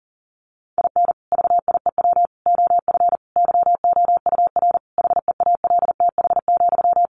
Morse code, heard as a keyed tone, about 37 wpm, 700 Hz. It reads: IN VIEW OF YOUR HEALTH,
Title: Sentences from Top 900 Words 40wpm